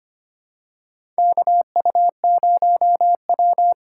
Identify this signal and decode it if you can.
Morse code, heard as a keyed tone, about 25 wpm, 700 Hz. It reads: KU0W
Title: USA Call Signs - 4 Characters - 3x Character Spacing - 25wpm